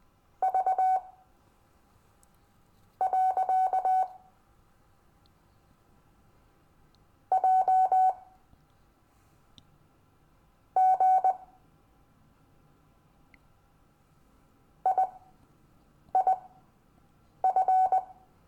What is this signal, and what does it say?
Morse code, heard as a keyed tone, about 20 wpm, spaced as for 6 wpm, 750 Hz. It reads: V. J G IIF
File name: cw / Lesson15 five forwardslash